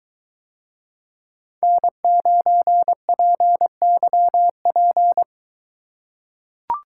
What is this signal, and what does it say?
Morse code, heard as a keyed tone, about 22 wpm, 700 Hz. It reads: N9PYP